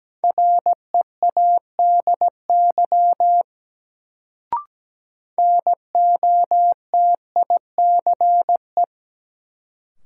Morse code, heard as a keyed tone, about 17 wpm, 700 Hz. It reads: READY NOTICE
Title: Sentences from Top 400 Words - New Word Drill - 17wpm